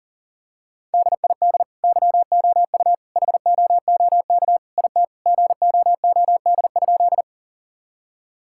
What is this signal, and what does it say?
Morse code, heard as a keyed tone, about 37 wpm, 700 Hz. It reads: DID YOU HOOK IT GOOD?
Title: Sentences from Top 1600 Words 40wpm